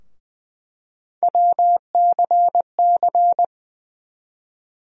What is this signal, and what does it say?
Morse code, heard as a keyed tone, about 21 wpm, 700 Hz. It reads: WCC
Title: USA Call Sign Postfixes - 3 Letters 20wpm